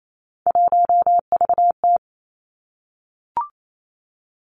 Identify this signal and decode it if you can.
Morse code, heard as a keyed tone, about 28 wpm, 700 Hz. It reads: 1VT
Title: Three Challenging Letter-Number Combinations 28wpm